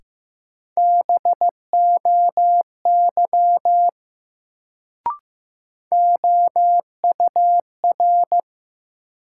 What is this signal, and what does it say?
Morse code, heard as a keyed tone, about 15 wpm, 700 Hz. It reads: BOY OUR
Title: Sentences from Top 200 Words - New Word Drill - 15wpm